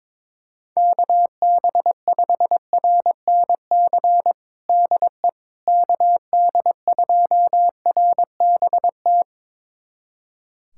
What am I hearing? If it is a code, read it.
Morse code, heard as a keyed tone, about 22 wpm, 700 Hz. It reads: KB5RNC DE KD2RBT
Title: USA Callsign DE Callsign - 6 Characters - 22wpm